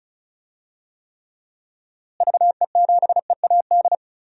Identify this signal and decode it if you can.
Morse code, heard as a keyed tone, about 36 wpm, 700 Hz. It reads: VE7EAD
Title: DXCC-100 Call Signs - 6 Characters - Repeated 3x - 35wpm